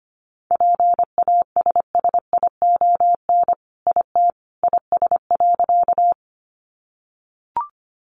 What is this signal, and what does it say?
Morse code, heard as a keyed tone, about 25 wpm, 700 Hz. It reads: PASSION IT IS.
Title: Sentences from Top 1300 Words 25wpm